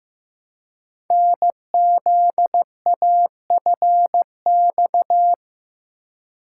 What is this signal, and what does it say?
Morse code, heard as a keyed tone, about 15 wpm, 700 Hz. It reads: NZAFX
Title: Code Groups - 5 Letters - 15wpm